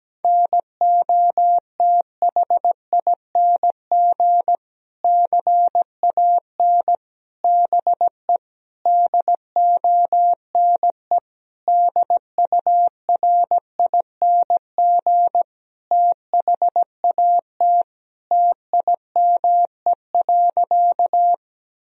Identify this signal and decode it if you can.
Morse code, heard as a keyed tone, about 17 wpm, 700 Hz. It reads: NOTHING CAN BE DONE DURING THAT TIME.
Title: Sentences from Top 1000 Words - Review Part 2 - 17wpm